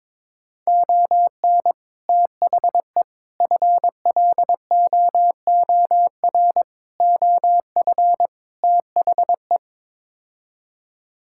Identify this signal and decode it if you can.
Morse code, heard as a keyed tone, about 21 wpm, 700 Hz. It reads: ON THE FLOOR OF THE
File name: practice / 22wpm / Build A Phrase - 5-Word Phrases - 22wpm